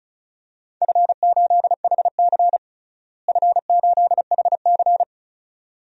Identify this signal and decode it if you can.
Morse code, heard as a keyed tone, about 36 wpm, 700 Hz. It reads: F8HC F8HC
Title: DXCC-100 Call Signs - 4 Characters - Repeated 2x - 35wpm